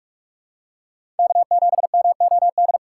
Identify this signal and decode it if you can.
Morse code, heard as a keyed tone, about 44 wpm, 700 Hz. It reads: K7MOD
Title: USA Call Signs - 5 Characters - 3x Character Spacing - 45wpm